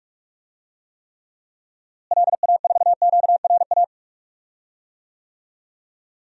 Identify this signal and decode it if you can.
Morse code, heard as a keyed tone, about 44 wpm, 700 Hz. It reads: LA4QRA
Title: DXCC-100 Call Signs - 6 Characters - Repeated 3x - 45wpm